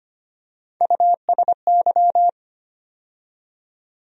Morse code, heard as a keyed tone, about 25 wpm, 700 Hz. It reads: USY
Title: USA Call Sign Postfixes - 3 Letters 25wpm